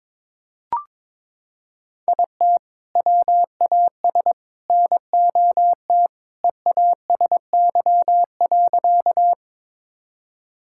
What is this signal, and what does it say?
Morse code, heard as a keyed tone, about 22 wpm, 700 Hz. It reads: IT WAS NOT EASY.
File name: practice / 22wpm / Sentences with 3-Word Phrases - 22wpm